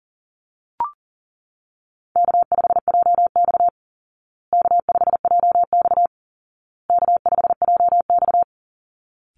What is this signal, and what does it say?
Morse code, heard as a keyed tone, about 40 wpm, 700 Hz. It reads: K5JX K5JX K5JX
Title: Call Signs Repeated 2x - 40wpm